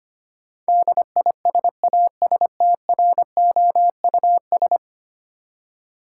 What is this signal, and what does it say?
Morse code, heard as a keyed tone, about 25 wpm, 700 Hz. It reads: DISASTROUS
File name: Ten Letter Words - 25wpm